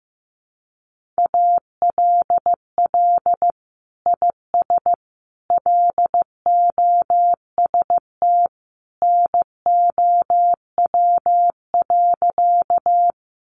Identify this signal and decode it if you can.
Morse code, heard as a keyed tone, about 15 wpm, 700 Hz. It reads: ALL IS LOST NOW.
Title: Sentences from Top 1400 Words 15wpm